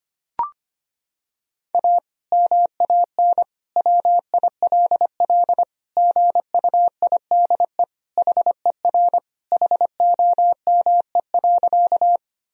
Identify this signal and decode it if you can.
Morse code, heard as a keyed tone, about 25 wpm, 700 Hz. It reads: A MAN WILL GUIDE HER HOME.
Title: Sentences from Top 1000 Words - Review Part 3 - 25wpm